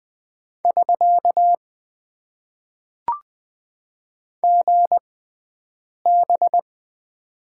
Morse code, heard as a keyed tone, about 20 wpm, spaced as for 7 wpm, 700 Hz. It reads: <SK> GB
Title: Top 90 QSO Elements - 6x Character Spacing - 20wpm